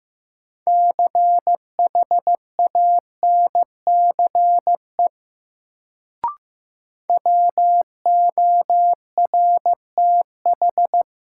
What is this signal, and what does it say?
Morse code, heard as a keyed tone, about 15 wpm, 700 Hz. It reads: CHANCE WORTH
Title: Sentences from Top 800 Words - New Word Drill - 15wpm